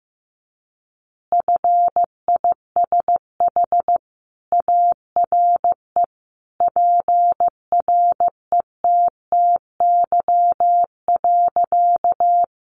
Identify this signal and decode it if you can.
Morse code, heard as a keyed tone, about 15 wpm, 700 Hz. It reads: FISH ARE PRETTY.
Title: Easy Phrases 15wpm